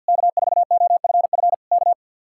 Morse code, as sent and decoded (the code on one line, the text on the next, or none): -.- ....- --- ..-. ..-. / -.-
K4OFF K